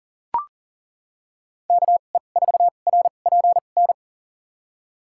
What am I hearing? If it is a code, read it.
Morse code, heard as a keyed tone, about 40 wpm, spaced as for 29 wpm, 700 Hz. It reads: KE4RPN
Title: USA Call Signs - 6 Characters - 2x Character Spacing - 40wpm